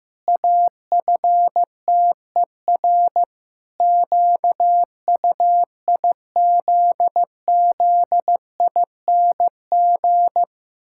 Morse code, heard as a keyed tone, about 15 wpm, 700 Hz. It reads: AFTER QUIZZING
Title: Pangram Challenge v2 - 15wpm